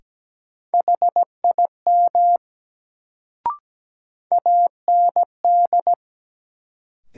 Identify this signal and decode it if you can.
Morse code, heard as a keyed tone, about 17 wpm, 700 Hz. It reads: HIM AND
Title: Sentences from Top 100 Words - New Word Drill - 17wpm